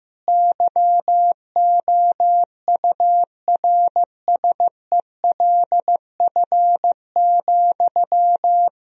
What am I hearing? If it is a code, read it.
Morse code, heard as a keyed tone, about 15 wpm, 700 Hz. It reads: YOURSELF,
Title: Sentences from Top 1000 Words - Review Part 3 - 15wpm